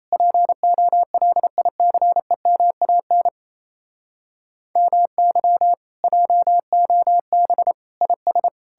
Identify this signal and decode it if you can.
Morse code, heard as a keyed tone, about 38 wpm, 700 Hz. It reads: POLICEMAN MY JOB IS